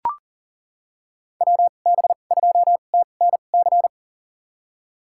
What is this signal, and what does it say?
Morse code, heard as a keyed tone, about 40 wpm, spaced as for 29 wpm, 700 Hz. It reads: WB2TNC
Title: USA Call Signs - 6 Characters - 2x Character Spacing - 40wpm